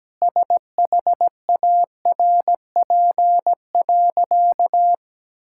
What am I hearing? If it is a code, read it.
Morse code, heard as a keyed tone, about 17 wpm, 700 Hz. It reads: SHARP.
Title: Sentences from Top 1600 Words 17wpm